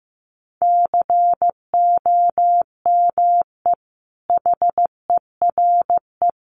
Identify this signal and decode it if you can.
Morse code, heard as a keyed tone, about 15 wpm, 700 Hz. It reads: COME HERE
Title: Sentences from Top 800 Words 15wpm